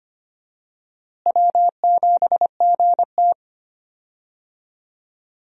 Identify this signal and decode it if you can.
Morse code, heard as a keyed tone, about 25 wpm, 700 Hz. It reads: W7GT
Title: USA Call Signs - 4 Characters - Repeated 3x - 25wpm